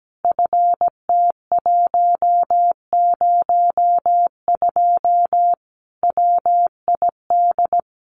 Morse code, heard as a keyed tone, about 17 wpm, 700 Hz. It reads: FT102 WID
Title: Common QSO Phrases 17wpm